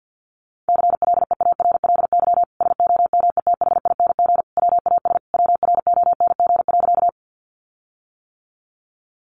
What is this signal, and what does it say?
Morse code, heard as a keyed tone, about 50 wpm, 700 Hz. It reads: CLEARLY SOMETHING WAS WRONG.
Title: Sentences from Top 1300 Words 50wpm